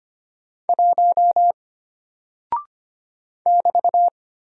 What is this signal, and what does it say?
Morse code, heard as a keyed tone, about 25 wpm, 700 Hz. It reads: 1 <BT>
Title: Single Character (Letter:Number:Punctuation:Prosign) Alt 25wpm